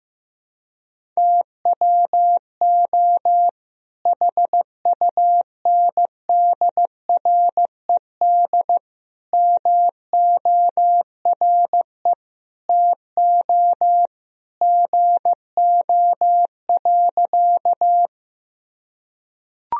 Morse code, heard as a keyed tone, about 15 wpm, 700 Hz. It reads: TWO HUNDRED MORE TO GO.
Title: Sentences from Top 1000 Words - Review Part 2 - 15wpm